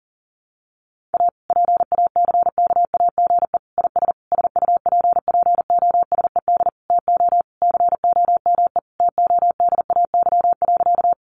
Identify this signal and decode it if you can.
Morse code, heard as a keyed tone, about 40 wpm, 700 Hz. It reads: A PACKAGE IS SUPPOSED TO COME TODAY.
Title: Additional Easy Phrases 40wpm